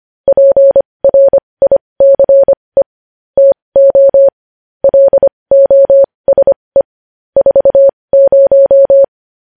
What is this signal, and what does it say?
Morse code, heard as a keyed tone, about 25 wpm, spaced as for 20 wpm, 550 Hz. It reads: PRICE TO LOSE 40
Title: News Headlines at 20